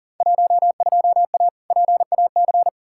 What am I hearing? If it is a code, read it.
Morse code, heard as a keyed tone, about 40 wpm, 700 Hz. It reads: 12A PAC
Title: Field Day Exchanges Encore - 40wpm